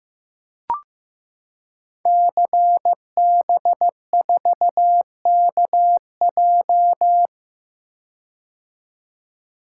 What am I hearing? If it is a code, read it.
Morse code, heard as a keyed tone, about 15 wpm, 700 Hz. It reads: CB4KJ
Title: Code Groups - 5 Letter:Numbers - 15wpm